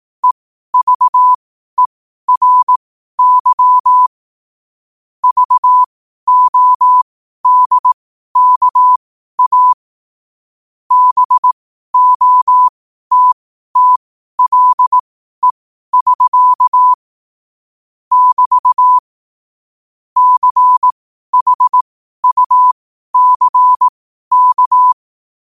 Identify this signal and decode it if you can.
Morse code, heard as a keyed tone, about 18 wpm, spaced as for 12 wpm, 1000 Hz. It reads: EVERY VODKA BOTTLE<SK> <BT> CHUCK
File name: qotdfiles / 12wpm